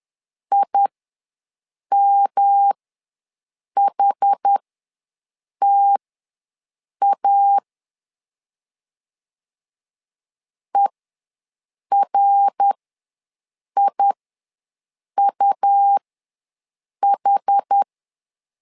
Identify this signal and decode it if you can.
Morse code, heard as a keyed tone, about 10 wpm, spaced as for 6 wpm, 800 Hz. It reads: IMHTA ERIUH
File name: LEZIONI / 3 MWRO / 3 MWR ws 10wpm 2151